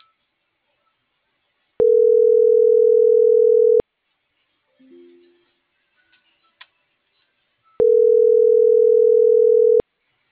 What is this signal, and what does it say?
Ringback tone (North America), 440 + 480 Hz.